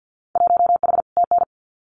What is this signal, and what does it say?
Morse code, heard as a keyed tone, about 50 wpm, 700 Hz. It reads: 1H TN
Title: Winter Field Day Exchanges 50wpm